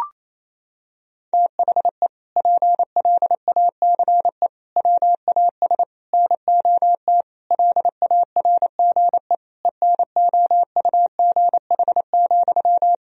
Morse code, heard as a keyed tone, about 28 wpm, 700 Hz. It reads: THE PLACE WAS NOT LARGE ENOUGH,
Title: Sentences from Top 1000 Words - Review Part 1 - 28wpm